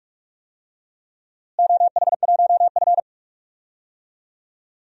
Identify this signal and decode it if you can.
Morse code, heard as a keyed tone, about 43 wpm, 700 Hz. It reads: OH1F
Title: DXCC-100 Call Signs - 4 Characters - Repeated 3x - 45wpm